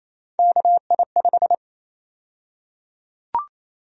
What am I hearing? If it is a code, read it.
Morse code, heard as a keyed tone, about 27 wpm, 700 Hz. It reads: KI5
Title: USA Call Sign Prefixes - 28wpm